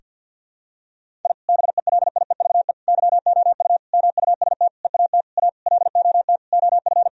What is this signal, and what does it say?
Morse code, heard as a keyed tone, about 52 wpm, 700 Hz. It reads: I BELIEVE YOU MUST EAT A LOT OF
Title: Sentences from Top 1000 Words - Review Part 3 - 50wpm